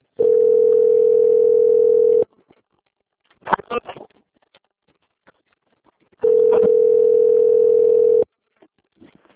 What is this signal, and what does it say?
Ringback tone (North America), 440 + 480 Hz.